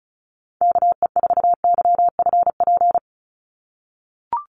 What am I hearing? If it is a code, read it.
Morse code, heard as a keyed tone, about 34 wpm, 700 Hz. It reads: KE4YFP
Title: USA Call Signs 35wpm